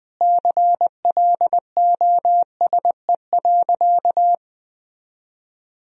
Morse code, heard as a keyed tone, about 20 wpm, 700 Hz. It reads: CLOSE.
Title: Sentences from Top 1000 Words - Review Part 1 - 20wpm